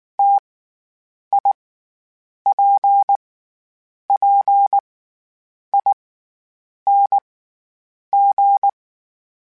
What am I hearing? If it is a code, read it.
Morse code, heard as a keyed tone, about 19 wpm, spaced as for 8 wpm, 800 Hz. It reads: TIPPING
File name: Difficult Combos / Difficult Combos-mp3s / T AOEI @18